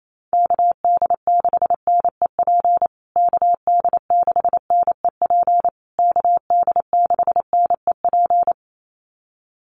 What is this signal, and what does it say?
Morse code, heard as a keyed tone, about 28 wpm, 700 Hz. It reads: KD6NEP KD6NEP KD6NEP
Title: USA Call Signs Repeated 3x 28wpm